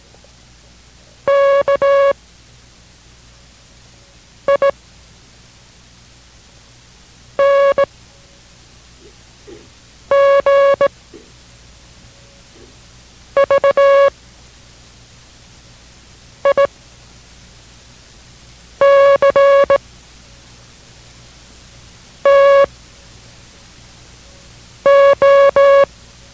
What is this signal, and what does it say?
Morse code, heard as a keyed tone, about 18 wpm, spaced as for 4 wpm, 560 Hz. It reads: KING VICTO